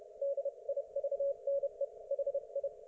Morse code, heard as a keyed tone, about 30 wpm, 560 Hz. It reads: DIVNE HI
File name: Archiv / CUC208 / ok1fhi cul208